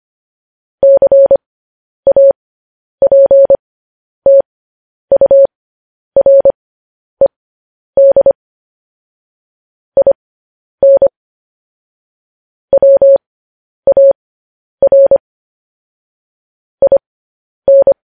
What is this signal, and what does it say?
Morse code, heard as a keyed tone, about 25 wpm, spaced as for 10 wpm, 550 Hz. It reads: CAPTURED IN WAR IN
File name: podcast / News headlines-1x-25x10